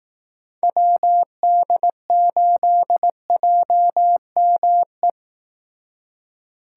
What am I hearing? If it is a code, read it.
Morse code, heard as a keyed tone, about 18 wpm, 700 Hz. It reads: WD8JME
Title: Speed Racing - USA 2x3 Call Signs 15wpm